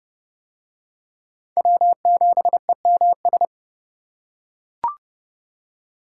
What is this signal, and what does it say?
Morse code, heard as a keyed tone, about 29 wpm, 700 Hz. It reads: W7EMS